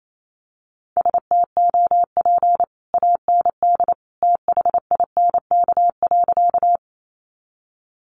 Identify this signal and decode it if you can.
Morse code, heard as a keyed tone, about 28 wpm, 700 Hz. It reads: STOP AND THINK.
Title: Sentences from Top 300 Words 28wpm